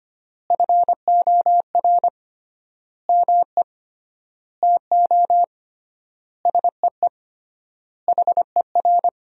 Morse code, heard as a keyed tone, about 25 wpm, 700 Hz. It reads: FOR ME TO SEE HER
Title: 3x Word-Spacing - Sentences from Top 500 Words - 25wpm